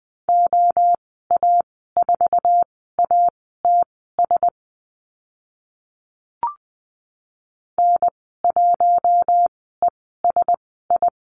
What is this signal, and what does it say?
Morse code, heard as a keyed tone, about 20 wpm, spaced as for 14 wpm, 700 Hz. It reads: OA4ATS N1ESI
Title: 062 Lesson 11g - Letter h - Call Signs 20wpm